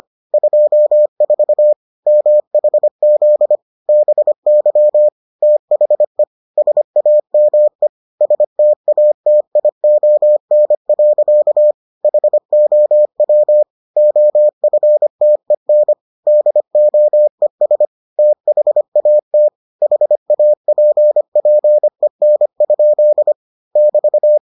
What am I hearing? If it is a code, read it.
Morse code, heard as a keyed tone, about 25 wpm, 600 Hz. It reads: 24 MHZ BY THE SAME STATION. HOW OFTEN DOES THAT HAPPEN? <BT>